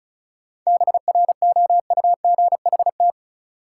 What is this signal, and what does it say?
Morse code, heard as a keyed tone, about 36 wpm, 700 Hz. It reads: BROUGHT
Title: Sentences from Top 500 Words - New Word Drill - 35wpm